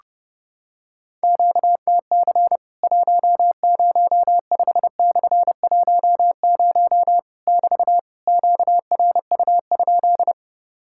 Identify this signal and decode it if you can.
Morse code, heard as a keyed tone, about 30 wpm, 700 Hz. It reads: QTC 105/10 <BT> QRU?
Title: Worked All Europe DX Contest - QTC Exchange - 30wpm